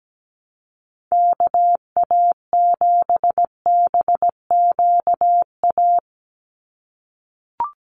Morse code, heard as a keyed tone, about 17 wpm, 700 Hz. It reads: KA7BQA
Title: USA Call Signs 17wpm